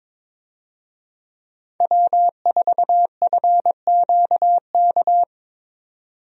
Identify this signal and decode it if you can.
Morse code, heard as a keyed tone, about 22 wpm, 700 Hz. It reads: W4FQK
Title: USA Call Signs - 5 Characters - 22wpm